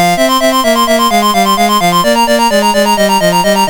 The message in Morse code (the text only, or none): none